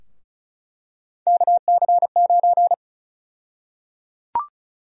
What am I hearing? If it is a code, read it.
Morse code, heard as a keyed tone, about 34 wpm, 700 Hz. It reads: KC9
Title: USA Call Sign Prefixes - Repeated 2x - 35wpm